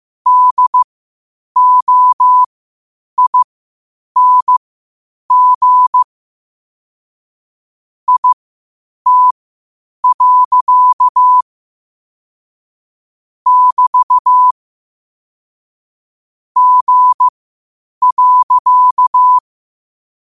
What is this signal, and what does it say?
Morse code, heard as a keyed tone, about 15 wpm, spaced as for 8 wpm, 1000 Hz. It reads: DOING IT. <BT> G.
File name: qotdfiles / 8wpm